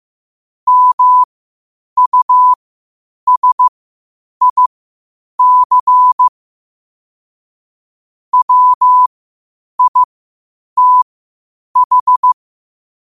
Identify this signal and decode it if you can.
Morse code, heard as a keyed tone, about 15 wpm, spaced as for 8 wpm, 1000 Hz. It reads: MUSIC WITH